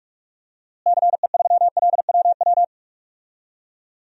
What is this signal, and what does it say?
Morse code, heard as a keyed tone, about 41 wpm, 700 Hz. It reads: CE3LWW
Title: DXCC-100 Call Signs - 6 Characters - Repeated 3x - 45wpm